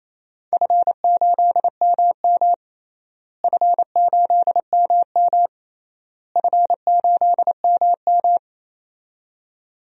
Morse code, heard as a keyed tone, about 27 wpm, 700 Hz. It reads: F8MM F8MM F8MM
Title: DXCC-100 Call Signs - 4 Characters - Repeated 3x - 28wpm